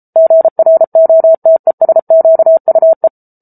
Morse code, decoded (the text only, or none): GROTESQUE